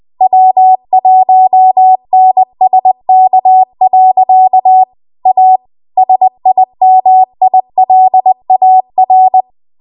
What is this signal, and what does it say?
Morse code, heard as a keyed tone, about 20 wpm, 750 Hz. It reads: W1NSK. A SIMILAR
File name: Morse / 20WPM